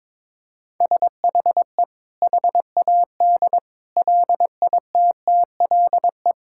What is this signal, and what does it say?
Morse code, heard as a keyed tone, about 21 wpm, 700 Hz. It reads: SHE HAD LITTLE
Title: Sentences from Top 1000 Words - Review Part 5 - 22wpm